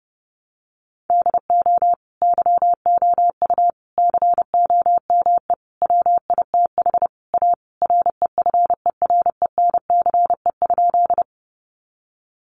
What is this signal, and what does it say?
Morse code, heard as a keyed tone, about 30 wpm, 700 Hz. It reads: DO YOU COME WITH A REFERENCE?
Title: Sentences from Top 1200 Words 30wpm